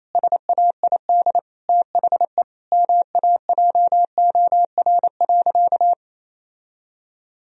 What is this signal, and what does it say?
Morse code, heard as a keyed tone, about 28 wpm, 700 Hz. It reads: SAID THE MAJOR.
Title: Sentences from Top 1000 Words - Review Part 4 - 28wpm